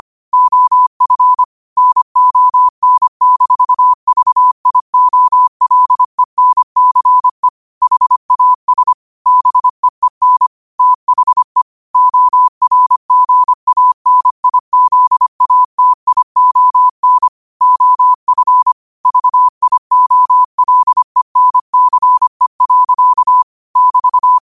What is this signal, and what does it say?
Morse code, heard as a keyed tone, about 25 wpm, 1000 Hz. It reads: OF NON-VIOLENCE HAS BEEN THE ORGANIZATION OF VIOLENCE. <BT>
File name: qotdfiles / 25wpm